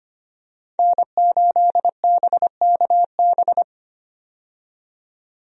Morse code, heard as a keyed tone, about 25 wpm, 700 Hz. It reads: N8BKB
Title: USA Call Signs - 5 Characters - 25wpm